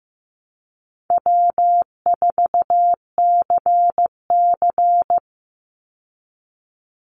Morse code, heard as a keyed tone, about 15 wpm, 700 Hz. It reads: W4CC
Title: Easy Call Signs Repeated 2x - 15wpm